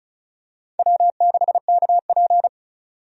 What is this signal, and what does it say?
Morse code, heard as a keyed tone, about 34 wpm, 700 Hz. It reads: W6KP
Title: USA Call Signs - 4 Characters - 35wpm